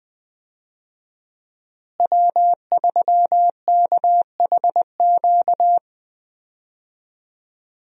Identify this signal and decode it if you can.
Morse code, heard as a keyed tone, about 20 wpm, 700 Hz. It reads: W3KHQ
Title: USA Call Signs - 5 Characters - Repeated 2x - 20wpm